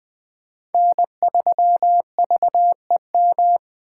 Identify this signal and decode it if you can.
Morse code, heard as a keyed tone, about 20 wpm, 700 Hz. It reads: N3VEM
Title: Speed Racing - USA 1x3 Call Signs 17wpm